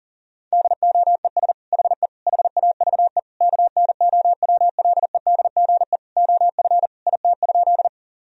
Morse code, heard as a keyed tone, about 40 wpm, 700 Hz. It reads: DOES HE HAVE KNOWLEDGE OF IT?
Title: Sentences from Top 1000 Words - Review Part 3 - 40wpm